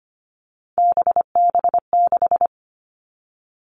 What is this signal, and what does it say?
Morse code, heard as a keyed tone, about 25 wpm, 700 Hz. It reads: BB6
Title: Three Challenging Letter-Number Combinations 25wpm